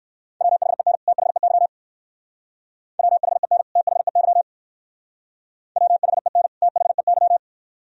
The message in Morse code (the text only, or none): WHEN THEY WHEN THEY WHEN THEY